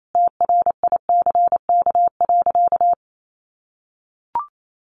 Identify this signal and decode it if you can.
Morse code, heard as a keyed tone, about 28 wpm, 700 Hz. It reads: TRICK.
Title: Sentences from Top 1500 Words 28wpm